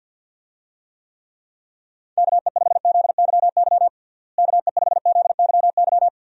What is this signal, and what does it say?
Morse code, heard as a keyed tone, about 50 wpm, 700 Hz. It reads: KE5ZYY KE5ZYY
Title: USA Call Signs - 6 Characters - Repeated 2x - 50wpm